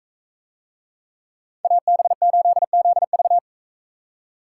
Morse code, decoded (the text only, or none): AB8ZV